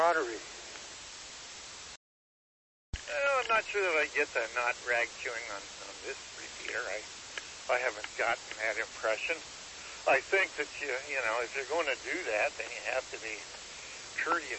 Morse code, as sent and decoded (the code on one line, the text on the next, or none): none